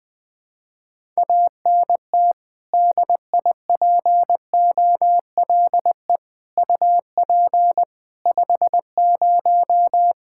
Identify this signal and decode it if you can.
Morse code, heard as a keyed tone, about 20 wpm, 700 Hz. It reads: ANT DIPOLE UP 50
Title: Practice QSO - Exchange 2 - Info - 20wpm